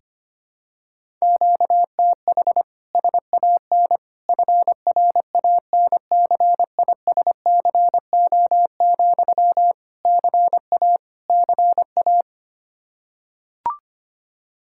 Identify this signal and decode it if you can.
Morse code, heard as a keyed tone, about 25 wpm, 700 Hz. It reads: QTH SAN FRANCISCO, CA CA
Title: K1USN SST QSO Practice 25wpm